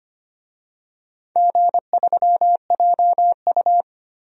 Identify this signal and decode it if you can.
Morse code, heard as a keyed tone, about 25 wpm, 700 Hz. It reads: G3JU
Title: DXCC-100 Call Signs - 4 Characters - Repeated 3x - 25wpm